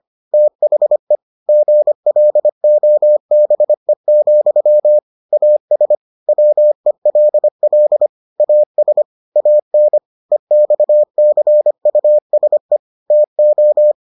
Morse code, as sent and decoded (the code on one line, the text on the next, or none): - .... / --. .-.. --- -... --..-- / .- ... / .-- . .-.. .-.. / .- ... / .- -. / . -..- -.-. ..- ... / - ---
THE GLOBE, AS WELL AS AN EXCUSE TO